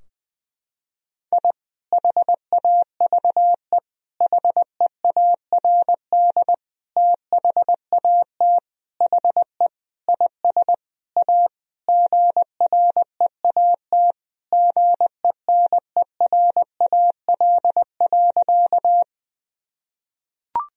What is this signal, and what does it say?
Morse code, heard as a keyed tone, about 20 wpm, 700 Hz. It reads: I HAVE HEARD THAT HE IS A GREAT GENERAL.
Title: Sentences from Top 1000 Words - Review Part 3 - 20wpm